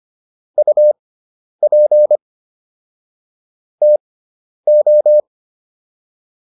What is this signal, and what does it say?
Morse code, heard as a keyed tone, about 25 wpm, spaced as for 10 wpm, 600 Hz. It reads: UP TO